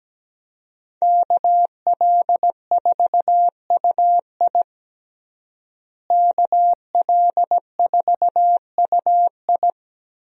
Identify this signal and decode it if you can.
Morse code, heard as a keyed tone, about 17 wpm, 700 Hz. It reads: KL4UI KL4UI
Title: USA Call Signs - 5 Characters - Repeated 3x - 17wpm